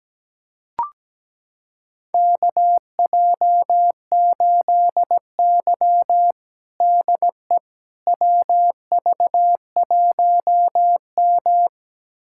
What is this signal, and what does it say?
Morse code, heard as a keyed tone, about 17 wpm, 700 Hz. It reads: KJ8Y DE WV1M
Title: USA Callsign DE Callsign - 4 Characters - 17wpm